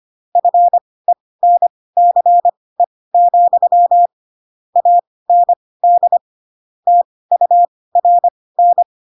Morse code, heard as a keyed tone, about 25 wpm, spaced as for 18 wpm, 700 Hz. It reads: FENCE, AND TURN
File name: Pinocchio / Pinocchio ch2 12 25 ik1hge